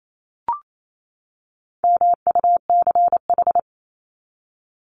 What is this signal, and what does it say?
Morse code, heard as a keyed tone, about 28 wpm, 700 Hz. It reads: MUCH
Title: Top 200 Words 28wpm